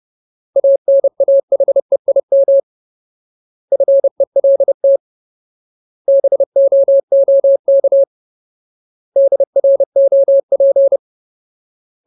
Morse code, heard as a keyed tone, about 30 wpm, 550 Hz. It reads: ANAHEIM FELT BOOK DROP
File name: Ultimate Word List / 30WPM Ultimate